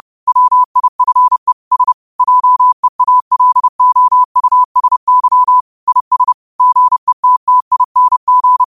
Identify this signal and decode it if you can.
Morse code, heard as a keyed tone, about 30 wpm, 1000 Hz. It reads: WIFE S JEALOUSY IS GETTING
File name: qotdfiles / 30wpm